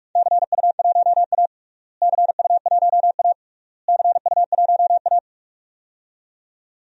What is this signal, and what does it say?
Morse code, heard as a keyed tone, about 46 wpm, 700 Hz. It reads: CU1A CU1A CU1A
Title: DXCC-100 Call Signs - 4 Characters - Repeated 3x - 45wpm